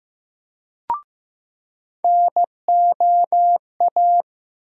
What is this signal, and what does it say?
Morse code, heard as a keyed tone, about 15 wpm, 700 Hz. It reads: NOA
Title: USA Call Sign Postfixes - 3 Letters 15wpm